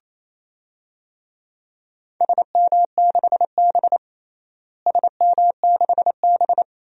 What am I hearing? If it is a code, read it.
Morse code, heard as a keyed tone, about 27 wpm, 700 Hz. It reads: SM6B SM6B
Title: DXCC-100 Call Signs - 4 Characters - Repeated 3x - 28wpm